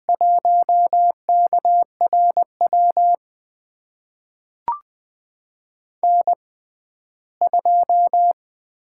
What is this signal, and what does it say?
Morse code, heard as a keyed tone, about 20 wpm, spaced as for 14 wpm, 700 Hz. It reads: 1KRW N2
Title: USA Call Signs - 5 Characters - 6x Character Spacing - 20wpm